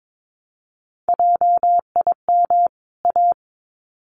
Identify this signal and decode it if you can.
Morse code, heard as a keyed tone, about 22 wpm, 700 Hz. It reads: JIM A
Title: CWT Exchange 22wpm